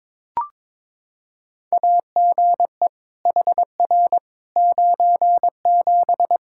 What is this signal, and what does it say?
Morse code, heard as a keyed tone, about 22 wpm, 700 Hz. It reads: AGE HR 97
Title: Practice QSO - Exchange 3 - More Info - 22wpm